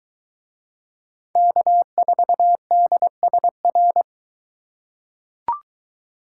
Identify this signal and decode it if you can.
Morse code, heard as a keyed tone, about 22 wpm, 700 Hz. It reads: K4DSR